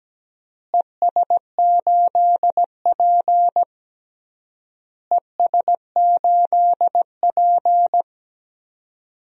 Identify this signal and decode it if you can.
Morse code, heard as a keyed tone, about 17 wpm, 700 Hz. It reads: ES8P ES8P
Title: DXCC-100 Call Signs - 4 Characters - Repeated 3x - 17wpm